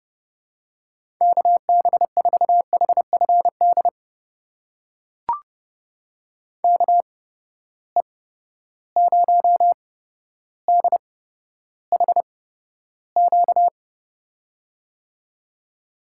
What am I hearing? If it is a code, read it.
Morse code, heard as a keyed tone, about 30 wpm, spaced as for 13 wpm, 700 Hz. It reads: KB4HFD KE0DHQ
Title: USA Call Signs - 6 Characters - 8x Character Spacing - 30wpm